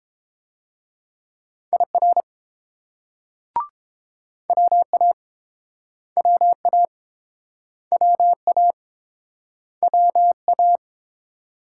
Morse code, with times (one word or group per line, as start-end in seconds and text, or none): 1.7-2.2 IR
4.5-5.1 WA
6.2-6.9 WA
7.9-8.7 WA
9.8-10.8 WA